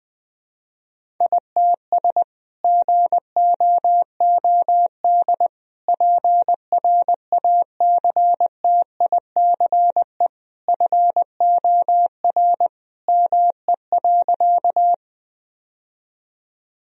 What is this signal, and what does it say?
Morse code, heard as a keyed tone, about 20 wpm, 700 Hz. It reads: ITS GOOD PRACTICE FOR ME.
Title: Sentences from Top 1000 Words - Review Part 4 - 20wpm